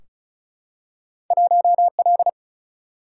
Morse code, heard as a keyed tone, about 34 wpm, 700 Hz. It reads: 1L
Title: Single-Letter Single-Number 35wpm